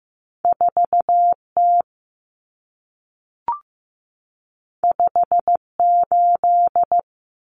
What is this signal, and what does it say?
Morse code, heard as a keyed tone, about 15 wpm, 700 Hz. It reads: 4T 58
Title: Common Cut Numbers - 2-Digit 15wpm